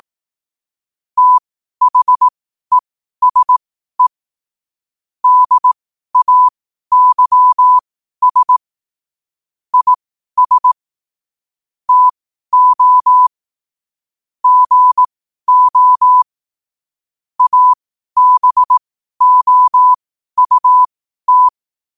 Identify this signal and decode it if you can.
Morse code, heard as a keyed tone, about 18 wpm, spaced as for 12 wpm, 1000 Hz. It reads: THESE DAYS IS TO GO ABOUT
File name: qotdfiles / 12wpm